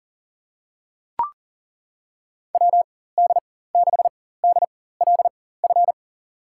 Morse code, heard as a keyed tone, about 38 wpm, spaced as for 18 wpm, 700 Hz. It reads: WD6DLF
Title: USA Call Signs - 6 Characters - 4x Character Spacing - 40wpm